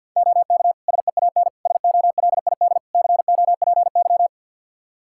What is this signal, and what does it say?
Morse code, heard as a keyed tone, about 50 wpm, 700 Hz. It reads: OK SEAN SOLID COPY
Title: Practice QSO - Exchange 3 - More Info - 50wpm